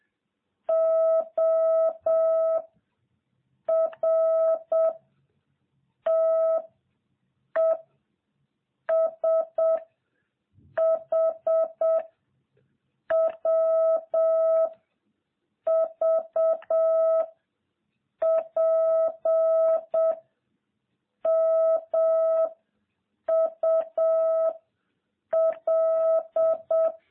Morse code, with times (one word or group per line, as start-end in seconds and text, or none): 0.7-2.6 O
3.7-4.9 R
6.1-6.6 T
7.6-7.8 E
8.9-9.8 S
10.8-12.0 H
13.1-14.7 W
15.7-17.2 V
18.2-20.1 P
21.3-22.5 M
23.3-24.5 U
25.3-26.9 L